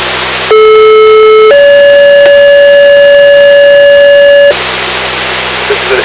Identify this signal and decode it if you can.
Two-tone page, heard as a one-second tone, then a three-second tone, about 430 then 570 Hz, with speech. It is heard from 0.5 to 4.5 s.